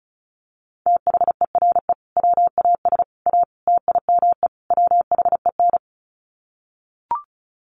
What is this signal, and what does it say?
Morse code, heard as a keyed tone, about 34 wpm, 700 Hz. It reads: THERE WAS A TIME WHEN
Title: Sets of 5 Words 35wpm